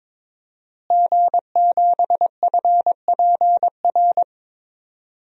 Morse code, heard as a keyed tone, about 22 wpm, 700 Hz. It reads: G7FPR
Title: DXCC-100 Call Signs - 5 Characters - Repeated 2x - 22wpm